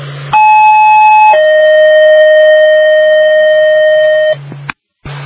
Two-tone page, heard as a one-second tone, then a three-second tone, about 860 then 620 Hz, with noise. It is heard from 0.3 to 4.3 s.